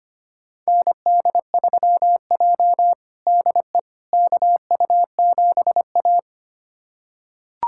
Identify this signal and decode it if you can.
Morse code, heard as a keyed tone, about 25 wpm, 700 Hz. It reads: ND3J DE KU7A
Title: USA Callsign DE Callsign - 4 Characters - 25wpm